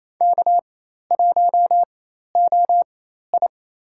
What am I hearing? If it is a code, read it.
Morse code, heard as a keyed tone, about 28 wpm, spaced as for 13 wpm, 700 Hz. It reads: K1OI
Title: USA Call Signs - 4 Characters - 4x Character Spacing - 28wpm